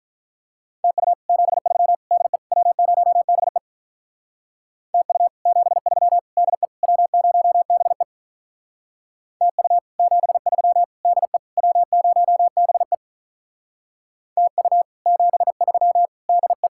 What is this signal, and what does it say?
Morse code, heard as a keyed tone, about 46 wpm, 700 Hz. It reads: TU 73 DE W0BE TU 73 DE W0BE TU 73 DE W0BE TU 73 DE